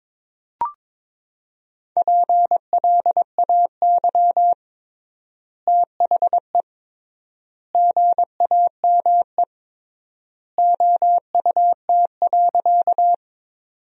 Morse code, heard as a keyed tone, about 22 wpm, 700 Hz. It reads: PLAY THE GAME OUT.
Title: 3x Word-Spacing - Sentences from Top 500 Words - 22wpm